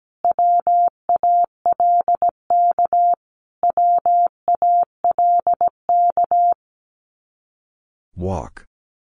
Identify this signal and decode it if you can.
Morse code, heard as a keyed tone, about 17 wpm, 700 Hz. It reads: WALK WALK
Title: Top 500 Words Repeated 17wpm